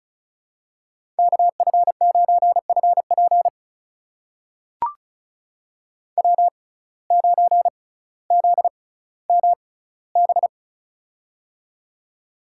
Morse code, heard as a keyed tone, about 36 wpm, spaced as for 18 wpm, 700 Hz. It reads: KF9FP W9ZMB
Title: USA Call Signs - 5 Characters - 6x Character Spacing - 35wpm